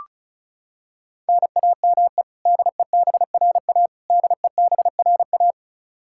Morse code, heard as a keyed tone, about 34 wpm, 700 Hz. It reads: NAME DEBRA DEBRA
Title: Practice QSO - Exchange 1 - Beginning - 35wpm